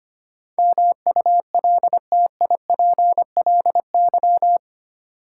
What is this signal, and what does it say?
Morse code, heard as a keyed tone, about 25 wpm, 700 Hz. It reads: MULTIPLY
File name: Eight Letter Words - 25wpm